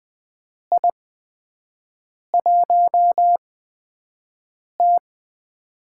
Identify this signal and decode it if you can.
Morse code, heard as a keyed tone, about 21 wpm, spaced as for 6 wpm, 700 Hz. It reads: I1T
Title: USA Call Signs - 8x Character Spacing - 20wpm